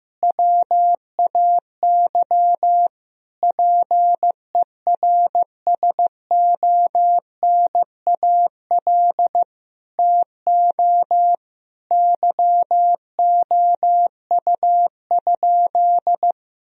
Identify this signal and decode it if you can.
Morse code, heard as a keyed tone, about 15 wpm, 700 Hz. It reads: WAY PERSONAL TO YOU?
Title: Sentences from Top 1000 Words - Review Part 4 - 15wpm